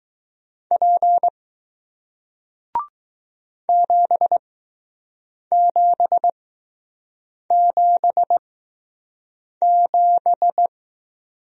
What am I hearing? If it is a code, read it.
Morse code, heard as a keyed tone, about 20 wpm, spaced as for 6 wpm, 700 Hz. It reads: P 7777